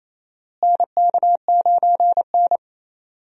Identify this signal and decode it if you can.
Morse code, heard as a keyed tone, about 28 wpm, 700 Hz. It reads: NK9N
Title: USA Call Signs - 4 Characters - Repeated 3x - 28wpm